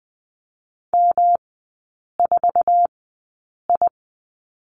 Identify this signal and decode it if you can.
Morse code, heard as a keyed tone, about 20 wpm, spaced as for 8 wpm, 700 Hz. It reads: M4I
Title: 107 Lesson 18e - Letter m - Three Characters 20wpm